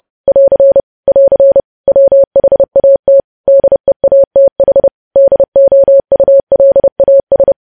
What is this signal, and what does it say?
Morse code, heard as a keyed tone, about 30 wpm, 550 Hz. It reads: <AR> <AR> WHAT DEATH DOULAS